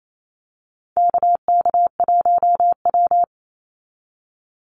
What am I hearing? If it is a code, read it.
Morse code, heard as a keyed tone, about 28 wpm, 700 Hz. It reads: KK1W
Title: NA SOTA Chaser Callsigns 28wpm